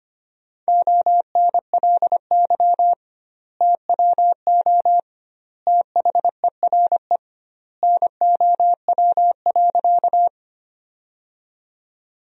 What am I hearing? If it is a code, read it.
Morse code, heard as a keyed tone, about 25 wpm, 700 Hz. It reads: ONLY TWO THERE NOW.
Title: 2x Word-Spacing - Sentences from Top 200 Words - 25wpm